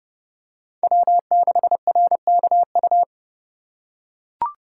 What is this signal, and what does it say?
Morse code, heard as a keyed tone, about 28 wpm, 700 Hz. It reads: W6RKU